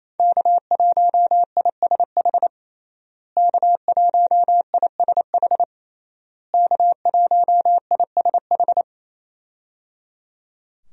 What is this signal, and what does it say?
Morse code, heard as a keyed tone, about 28 wpm, 700 Hz. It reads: K1ISH K1ISH K1ISH
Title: USA Call Signs - 5 Characters - Repeated 3x - 28wpm